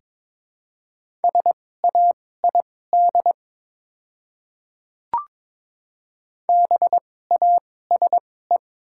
Morse code, heard as a keyed tone, about 22 wpm, spaced as for 16 wpm, 700 Hz. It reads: SAID BASE
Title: 2x Character Spacing - Top 300 Words - 22wpm